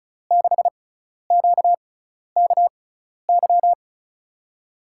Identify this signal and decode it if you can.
Morse code, heard as a keyed tone, about 36 wpm, spaced as for 12 wpm, 700 Hz. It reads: 6QKY
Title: USA Call Signs - 6 Characters - 6x Character Spacing - 35wpm